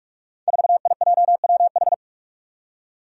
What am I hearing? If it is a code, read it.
Morse code, heard as a keyed tone, about 44 wpm, 700 Hz. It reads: 4IJWH
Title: Five Challenging Letter-Number Combinations 45wpm